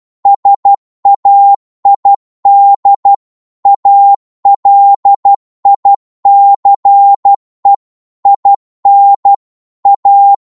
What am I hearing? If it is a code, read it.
Morse code, heard as a keyed tone, about 12 wpm, 800 Hz. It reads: SAID ALICE IN A
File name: Morse / psg-12wpm